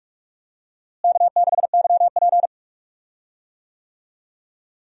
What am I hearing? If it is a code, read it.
Morse code, heard as a keyed tone, about 41 wpm, 700 Hz. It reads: K6YP
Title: USA Call Signs - 4 Characters - Repeated 2x - 45wpm